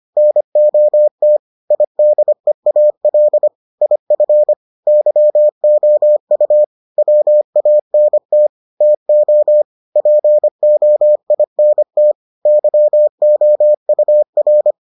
Morse code, heard as a keyed tone, about 25 wpm, 600 Hz. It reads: NOT IDEAL IF YOU WANT TO POINT YOUR